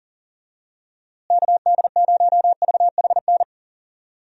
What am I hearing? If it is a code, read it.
Morse code, heard as a keyed tone, about 38 wpm, 700 Hz. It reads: KD0VHN
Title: USA Call Signs - 6 Characters - Repeated 2x - 40wpm